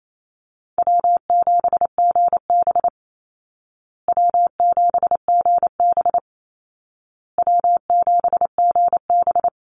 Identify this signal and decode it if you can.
Morse code, heard as a keyed tone, about 28 wpm, 700 Hz. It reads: W7GB W7GB W7GB
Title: Easy Call Signs Repeated 3x - 28wpm